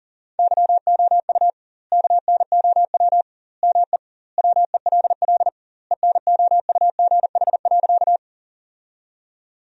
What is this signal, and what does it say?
Morse code, heard as a keyed tone, about 40 wpm, 700 Hz. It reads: YOU KNOW ME WELL ENOUGH.
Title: 2x Word-Spacing - Sentences from Top 400 Words - 40wpm